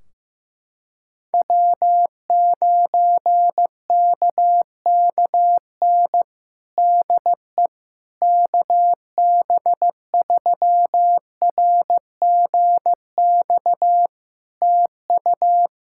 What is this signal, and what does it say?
Morse code, heard as a keyed tone, about 16 wpm, 700 Hz. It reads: W9KKN DE KB3RGX TU
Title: Practice QSO - Exchange 4 - Ending - 15wpm